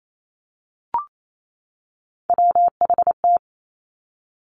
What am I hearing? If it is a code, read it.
Morse code, heard as a keyed tone, about 28 wpm, 700 Hz. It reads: WHT
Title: Three Challenging Letter-Number Combinations 28wpm